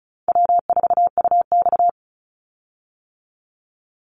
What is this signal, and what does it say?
Morse code, heard as a keyed tone, about 34 wpm, 700 Hz. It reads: W4UX
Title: Call Signs - 35wpm